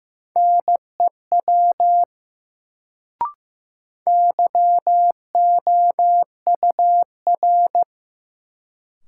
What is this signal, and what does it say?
Morse code, heard as a keyed tone, about 15 wpm, 700 Hz. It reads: NEW YOUR
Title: Sets of 2 Words - New Word Drill - 15wpm